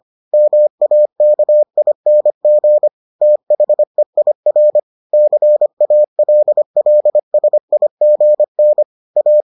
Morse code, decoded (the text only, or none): MAKING THEIR CALLSIGN A